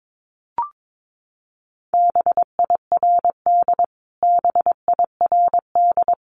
Morse code, heard as a keyed tone, about 22 wpm, 700 Hz. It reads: BIRD BIRD
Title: Top 500 Words Repeated 22wpm